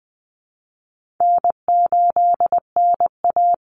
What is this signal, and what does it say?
Morse code, heard as a keyed tone, about 19 wpm, 700 Hz. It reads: N8NA
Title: Easy Call Signs Repeated 3x - 20wpm